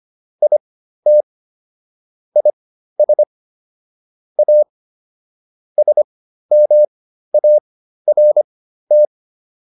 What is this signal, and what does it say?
Morse code, heard as a keyed tone, about 25 wpm, spaced as for 13 wpm, 600 Hz. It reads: IT IS A SMART